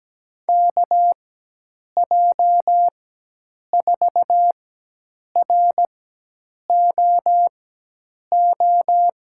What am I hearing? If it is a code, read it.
Morse code, heard as a keyed tone, about 17 wpm, spaced as for 8 wpm, 700 Hz. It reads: KJ4ROO
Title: USA Call Signs - 6 Characters - 4x Character Spacing - 17wpm